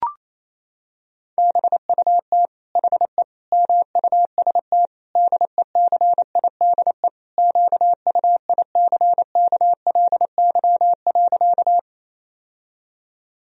Morse code, heard as a keyed tone, about 28 wpm, 700 Hz. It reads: BUT HE MUST DECIDE QUICKLY.
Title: Sentences from Top 1000 Words - Review Part 3 - 28wpm